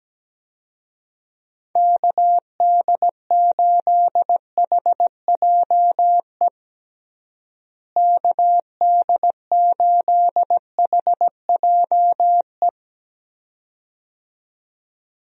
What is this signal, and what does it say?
Morse code, heard as a keyed tone, about 17 wpm, 700 Hz. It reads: KD8HJE KD8HJE
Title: USA Call Signs - 6 Characters - Repeated 2x - 17wpm